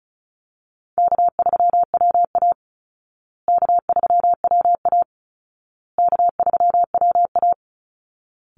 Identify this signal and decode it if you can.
Morse code, heard as a keyed tone, about 34 wpm, 700 Hz. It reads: K3WA K3WA K3WA
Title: Easy Call Signs Repeated 3x - 35wpm